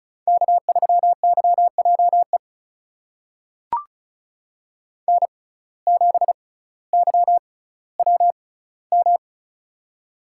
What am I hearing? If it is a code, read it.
Morse code, heard as a keyed tone, about 34 wpm, spaced as for 18 wpm, 700 Hz. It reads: K3YJE N7YWM
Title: USA Call Signs - 5 Characters - 6x Character Spacing - 35wpm